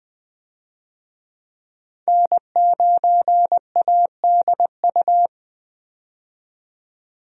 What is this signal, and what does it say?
Morse code, heard as a keyed tone, about 21 wpm, 700 Hz. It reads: N9ADU
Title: USA Call Signs - 5 Characters - 20wpm